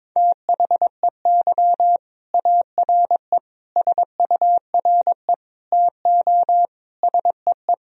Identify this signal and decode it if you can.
Morse code, heard as a keyed tone, about 22 wpm, 700 Hz. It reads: THEY ARE SURE TO SEE
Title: Sentences from Top 1000 Words - Review Part 1 - 22wpm